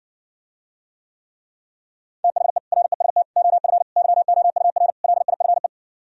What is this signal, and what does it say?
Morse code, heard as a keyed tone, about 69 wpm, 700 Hz. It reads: THE REST OF YOUR LIFE